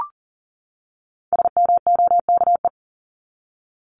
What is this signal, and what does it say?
Morse code, heard as a keyed tone, about 38 wpm, 700 Hz. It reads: SMOKE
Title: Top 1501-1750 Words 40wpm